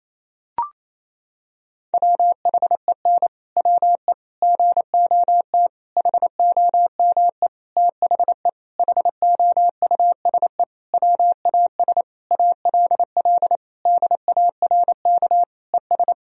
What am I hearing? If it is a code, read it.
Morse code, heard as a keyed tone, about 28 wpm, 700 Hz. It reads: WHEN WE GOT HOME THE HOUSE WAS ALL DARK ES
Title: Sentences from Top 500 Words and Abbreviations 28wpm